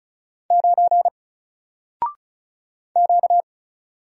Morse code, heard as a keyed tone, about 36 wpm, 700 Hz. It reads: 9 Q
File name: Single-Letter-Number-35wpm